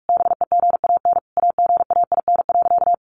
Morse code, heard as a keyed tone, about 44 wpm, 700 Hz. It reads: BEGAN AGAIN.
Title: Sentences from Top 1300 Words 45wpm